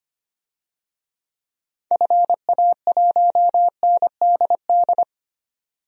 Morse code, heard as a keyed tone, about 25 wpm, 700 Hz. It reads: FA1NDD
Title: DXCC-100 Call Signs - 6 Characters - 25wpm